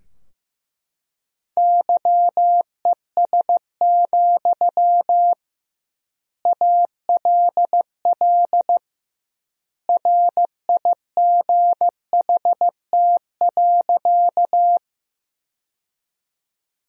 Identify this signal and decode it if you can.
Morse code, heard as a keyed tone, about 16 wpm, 700 Hz. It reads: YES, ALL RIGHT.
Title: 2x Word-Spacing - Sentences from Top 500 Words - 15wpm